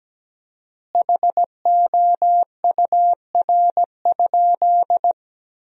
Morse code, heard as a keyed tone, about 17 wpm, 700 Hz. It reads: HOUR?
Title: 4x Word-Spacing - Sentences from Top 400 Words - 17wpm